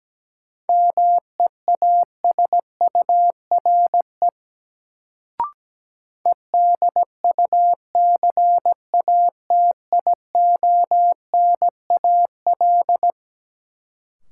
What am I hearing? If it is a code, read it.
Morse code, heard as a keyed tone, about 17 wpm, 700 Hz. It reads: MEASURE EDUCATIONAL
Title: Sentences from Top 1200 Words - New Word Drill Alt - 17wpm